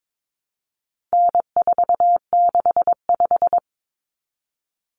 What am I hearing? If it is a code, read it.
Morse code, heard as a keyed tone, about 22 wpm, 700 Hz. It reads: N465
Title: Common Cut Numbers - 4-Digit 22wpm